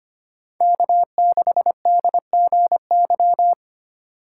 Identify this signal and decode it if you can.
Morse code, heard as a keyed tone, about 25 wpm, 700 Hz. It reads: K6DGY
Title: USA Call Signs - 5 Characters - 3x Character Spacing - 25wpm